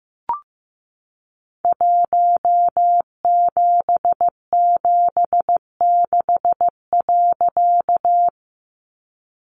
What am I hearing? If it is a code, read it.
Morse code, heard as a keyed tone, about 15 wpm, 700 Hz. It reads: 1776.
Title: A Sound Of Thunder 15wpm